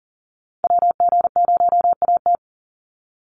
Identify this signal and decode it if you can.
Morse code, heard as a keyed tone, about 39 wpm, 700 Hz. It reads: WG0AT
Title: NA SOTA Chaser Callsigns 40wpm